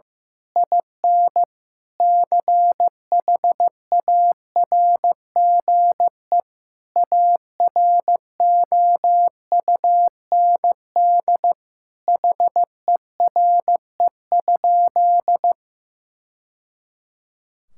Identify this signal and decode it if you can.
Morse code, heard as a keyed tone, about 14 wpm, 700 Hz. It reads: IN CHARGE AROUND HERE?
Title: Sentences from Top 1000 Words - Review Part 6 - 15wpm